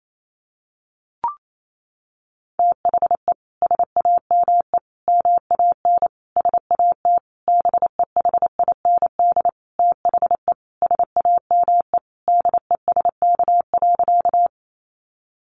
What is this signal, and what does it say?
Morse code, heard as a keyed tone, about 28 wpm, 700 Hz. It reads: THE SAME MAN SAT BEHIND THE SAME DESK.
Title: A Sound Of Thunder 28wpm